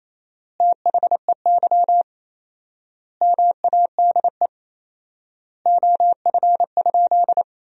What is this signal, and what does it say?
Morse code, heard as a keyed tone, about 28 wpm, 700 Hz. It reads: THEY MADE OF?
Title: 4x Word-Spacing - Sentences from Top 200 Words - 28wpm